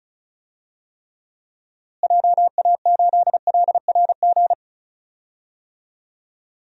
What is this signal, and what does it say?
Morse code, heard as a keyed tone, about 36 wpm, 700 Hz. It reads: JA8LRG
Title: DXCC-100 Call Signs - 6 Characters - Repeated 3x - 35wpm